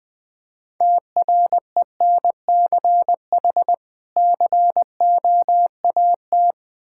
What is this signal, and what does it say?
Morse code, heard as a keyed tone, about 21 wpm, 700 Hz. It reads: TRENCH COAT
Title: Difficult Words with Hints - 20wpm